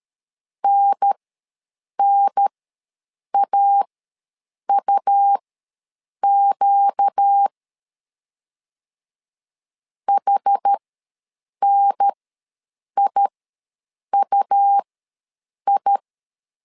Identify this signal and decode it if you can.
Morse code, heard as a keyed tone, about 13 wpm, spaced as for 7 wpm, 800 Hz. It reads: NNAUQ HNIUI
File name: LEZIONI / 4 Q ws 12wpm 1912